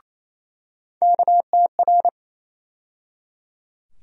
Morse code, heard as a keyed tone, about 28 wpm, 700 Hz. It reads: KTR
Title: USA Call Sign Postfixes - 3 Letters 28wpm